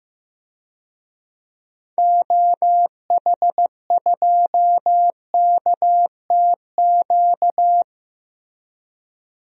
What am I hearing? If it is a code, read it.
Morse code, heard as a keyed tone, about 15 wpm, 700 Hz. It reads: OH2KTQ
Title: DXCC-100 Call Signs - 6 Characters - 15wpm